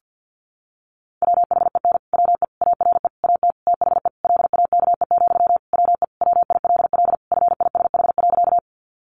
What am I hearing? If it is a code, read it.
Morse code, heard as a keyed tone, about 50 wpm, 700 Hz. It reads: WHEN WE ARE AT THE LAKE, WE WILL FISH.
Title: Sentences from Top 1100 Words 50wpm